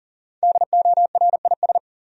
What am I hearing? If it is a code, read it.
Morse code, heard as a keyed tone, about 40 wpm, 700 Hz. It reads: DORIS
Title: USA Common Names - 40wpm